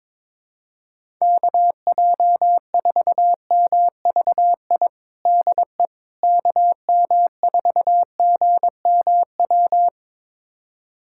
Morse code, heard as a keyed tone, about 22 wpm, 700 Hz. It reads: KJ4MVI DE KM4GMW
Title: USA Callsign DE Callsign - 6 Characters - 22wpm